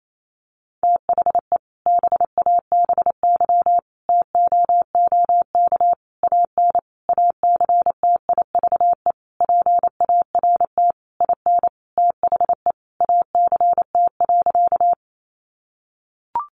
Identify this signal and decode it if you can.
Morse code, heard as a keyed tone, about 28 wpm, 700 Hz. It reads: THE BABY TOOK AN ACTIVE PART IN THE ACT.
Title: Sentences from Top 1300 Words 28wpm